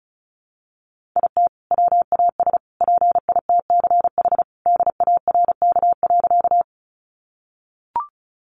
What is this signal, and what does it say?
Morse code, heard as a keyed tone, about 34 wpm, 700 Hz. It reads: IT WAS PITCH DARK.
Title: Sentences from Top 1300 Words 35wpm